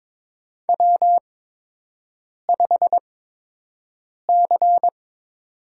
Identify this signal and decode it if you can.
Morse code, heard as a keyed tone, about 22 wpm, spaced as for 6 wpm, 700 Hz. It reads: W5C
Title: USA Call Signs - 4 Characters - 8x Character Spacing - 22wpm